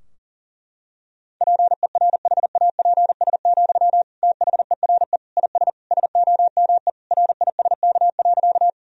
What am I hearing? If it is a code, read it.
Morse code, heard as a keyed tone, about 40 wpm, 700 Hz. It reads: PERHAPS, THERE IS SOME RISK.
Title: Sentences from Top 1000 Words - Review Part 4 - 40wpm